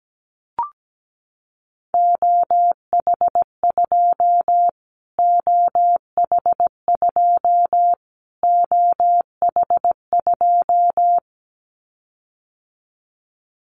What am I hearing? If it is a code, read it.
Morse code, heard as a keyed tone, about 17 wpm, 700 Hz. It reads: OH2 OH2 OH2
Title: Callsign Prefixes for Top 25 DXCC - Repeated 3x - 17wpm